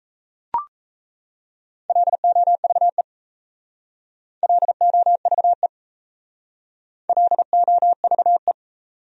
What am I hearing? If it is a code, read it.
Morse code, heard as a keyed tone, about 37 wpm, 700 Hz. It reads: LOVE LOVE LOVE